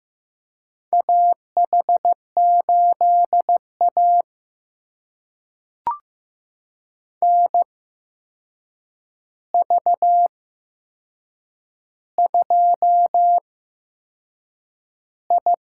Morse code, heard as a keyed tone, about 15 wpm, spaced as for 6 wpm, 700 Hz. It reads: AH8A NV2I
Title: USA Call Signs - 4 Characters - 8x Character Spacing - 15wpm